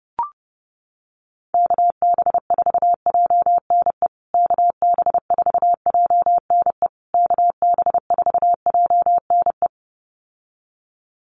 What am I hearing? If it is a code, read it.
Morse code, heard as a keyed tone, about 30 wpm, 700 Hz. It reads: KB4JNE KB4JNE KB4JNE
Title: USA Call Signs Repeated 3x 30wpm